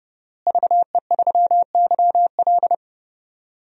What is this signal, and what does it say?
Morse code, heard as a keyed tone, about 30 wpm, 700 Hz. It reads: VE3YL
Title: DXCC-100 Call Signs - 5 Characters - Repeated 3x - 30wpm